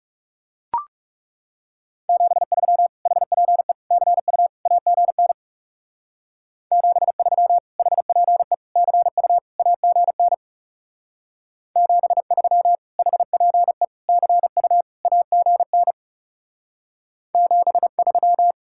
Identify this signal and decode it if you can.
Morse code, heard as a keyed tone, about 38 wpm, 700 Hz. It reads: 73 HPE CU AGN 73 HPE CU AGN 73 HPE CU AGN 73